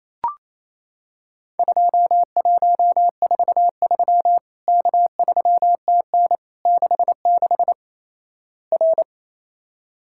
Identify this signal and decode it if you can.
Morse code, heard as a keyed tone, about 28 wpm, 700 Hz. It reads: 2143 K3TN 66 R
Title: Worked All Europe DX Contest - QTC Exchange - 28wpm